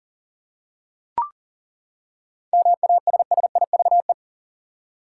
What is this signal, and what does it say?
Morse code, heard as a keyed tone, about 40 wpm, 700 Hz. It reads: MASSIVE
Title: Seven Letter Words - 40wpm